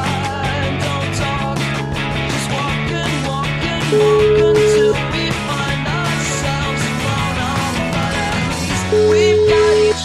Ringback tone (Europe, 425 Hz), with noise.